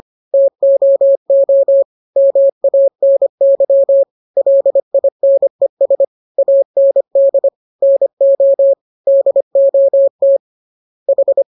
Morse code, heard as a keyed tone, about 25 wpm, 550 Hz. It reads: TOO MANY LINES AND NO DOT H